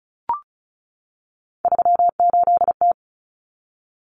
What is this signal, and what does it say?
Morse code, heard as a keyed tone, about 34 wpm, 700 Hz. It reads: 38T
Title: Common Cut Numbers - 3-Digit 35wpm